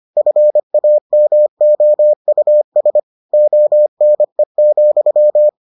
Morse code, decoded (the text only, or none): FAMOUS ONE,